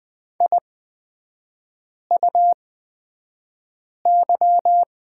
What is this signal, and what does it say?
Morse code, heard as a keyed tone, about 20 wpm, spaced as for 5 wpm, 700 Hz. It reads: IUY
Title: 255 Lesson 41e - Period - Three Characters 20wpm